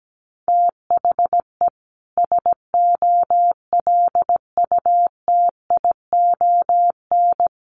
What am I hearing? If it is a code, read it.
Morse code, heard as a keyed tone, about 17 wpm, 700 Hz. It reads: THE SOLUTION
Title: Sentences from Top 1300 Words 17wpm